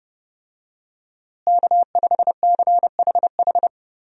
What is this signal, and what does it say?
Morse code, heard as a keyed tone, about 32 wpm, 700 Hz. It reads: K5CHH
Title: USA Call Signs - 5 Characters - Repeated 3x - 30wpm